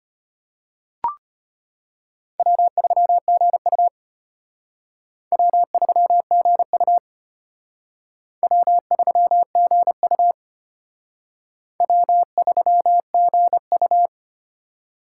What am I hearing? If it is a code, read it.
Morse code, heard as a keyed tone, about 31 wpm, 700 Hz. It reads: W3GU W3GU W3GU W3GU
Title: Speed Racing - USA 1x2 Call Signs 25wpm